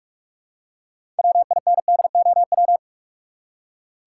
Morse code, heard as a keyed tone, about 43 wpm, 700 Hz. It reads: WINDOW
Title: Six Letter Words - 45wpm